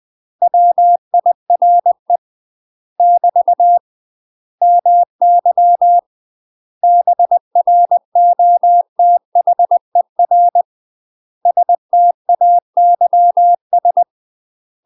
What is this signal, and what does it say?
Morse code, heard as a keyed tone, about 20 wpm, 700 Hz. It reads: WIRE <BT> MY BROTHER STAYS
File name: UK MASTER QSOs TRAINING / 20WPM / UK MASTER QSOs 0058